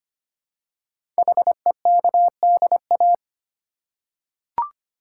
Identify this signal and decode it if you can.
Morse code, heard as a keyed tone, about 25 wpm, 700 Hz. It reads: HEKDA
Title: Code Groups - 5 Letters - 25wpm